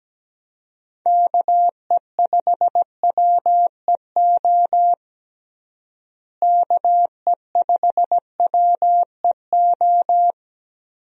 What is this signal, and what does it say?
Morse code, heard as a keyed tone, about 17 wpm, 700 Hz. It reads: KE5WEO KE5WEO
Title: USA Call Signs - 6 Characters - Repeated 2x - 17wpm